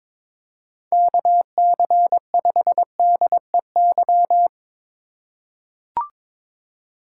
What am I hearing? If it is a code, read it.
Morse code, heard as a keyed tone, about 22 wpm, 700 Hz. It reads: KC5DEY
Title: USA Call Signs - 2x Character Spacing - 22wpm